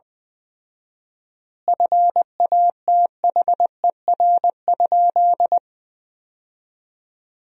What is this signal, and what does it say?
Morse code, heard as a keyed tone, about 20 wpm, 700 Hz. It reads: FATHER?
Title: 4x Word-Spacing - Sentences from Top 400 Words - 20wpm